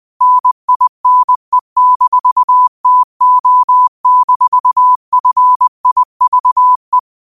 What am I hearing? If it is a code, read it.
Morse code, heard as a keyed tone, about 20 wpm, 1000 Hz. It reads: NINE-TO-FIVE
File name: qotdfiles / 20wpm